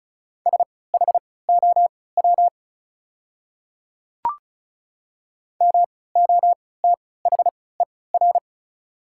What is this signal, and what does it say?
Morse code, heard as a keyed tone, about 36 wpm, spaced as for 20 wpm, 700 Hz. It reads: SHOW MOTHER
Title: 3x Character Spacing - Top 300 Words - 35wpm